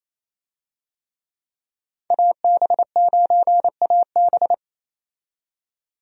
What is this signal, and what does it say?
Morse code, heard as a keyed tone, about 28 wpm, 700 Hz. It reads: AB9AB
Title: USA Call Signs - 5 Characters - 28wpm